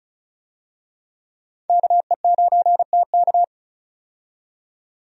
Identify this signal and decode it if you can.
Morse code, heard as a keyed tone, about 34 wpm, 700 Hz. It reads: KE9TK
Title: USA Call Signs - 5 Characters - 35wpm